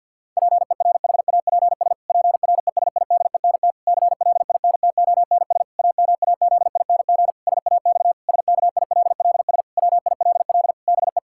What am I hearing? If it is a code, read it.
Morse code, heard as a keyed tone, about 50 wpm, 700 Hz. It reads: PERHAPS PRESIDENT CLINTONS AMAZING SAX SKILLS WILL BE
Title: Pangram Challenge v2 - 50wpm